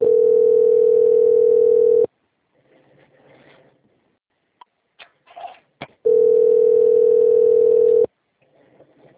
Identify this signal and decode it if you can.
Ringback tone (North America), 440 + 480 Hz.